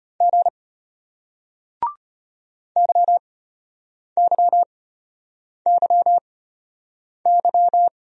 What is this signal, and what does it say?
Morse code, heard as a keyed tone, about 32 wpm, spaced as for 8 wpm, 700 Hz. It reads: G YYYY